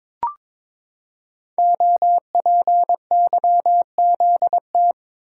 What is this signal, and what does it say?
Morse code, heard as a keyed tone, about 23 wpm, 700 Hz. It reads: OPYZT
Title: Code Groups - 5 Letters - 22wpm